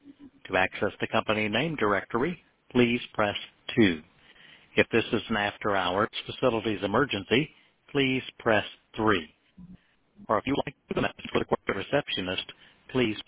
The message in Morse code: none